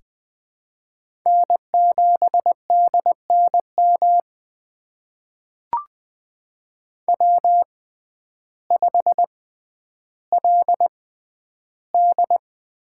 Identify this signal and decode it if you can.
Morse code, heard as a keyed tone, about 20 wpm, spaced as for 11 wpm, 700 Hz. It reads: N7DNM W5LD
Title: USA Call Signs - 6x Character Spacing - 20wpm